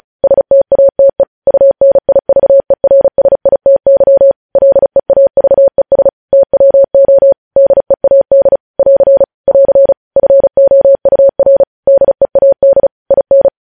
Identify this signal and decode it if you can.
Morse code, heard as a keyed tone, about 36 wpm, 550 Hz. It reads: STATE UNIVERSITY LEAVES TWO DEAD <AR> <AR> FOUR DEAD IN
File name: podcast / News headlines-1x-35x35